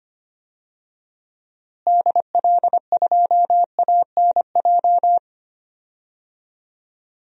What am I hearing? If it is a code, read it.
Morse code, heard as a keyed tone, about 25 wpm, 700 Hz. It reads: DL2ANJ
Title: DXCC-100 Call Signs - 6 Characters - Repeated 3x - 25wpm